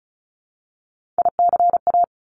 Morse code, heard as a keyed tone, about 36 wpm, 700 Hz. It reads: ICA
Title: Top Three Letter Combinations in Words 35wpm